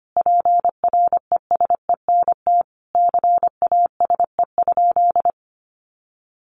Morse code, heard as a keyed tone, about 25 wpm, 700 Hz. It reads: PRESENT CASE?
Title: Sentences from Top 1200 Words 25wpm